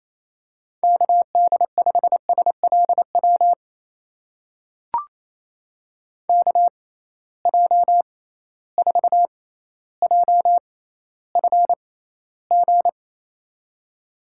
Morse code, heard as a keyed tone, about 28 wpm, spaced as for 14 wpm, 700 Hz. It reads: KD5SLW KJ4JFG
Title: USA Call Signs - 6 Characters - 6x Character Spacing - 28wpm